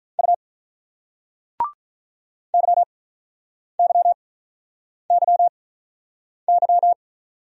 Morse code, heard as a keyed tone, about 44 wpm, spaced as for 8 wpm, 700 Hz. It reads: U YYYY